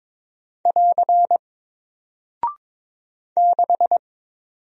Morse code, heard as a keyed tone, about 22 wpm, 700 Hz. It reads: <AR> 6
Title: Single Character (Letter:Number:Punctuation:Prosign) 22wpm